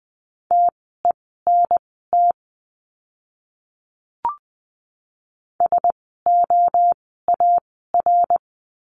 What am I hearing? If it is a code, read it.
Morse code, heard as a keyed tone, about 20 wpm, spaced as for 14 wpm, 700 Hz. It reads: TENT SOAR
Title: 054 Lesson 10f - Letter r - Words 20wpm